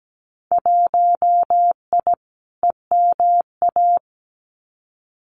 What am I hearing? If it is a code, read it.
Morse code, heard as a keyed tone, about 17 wpm, 700 Hz. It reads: 1I EMA
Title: Winter Field Day Exchanges Encore 17wpm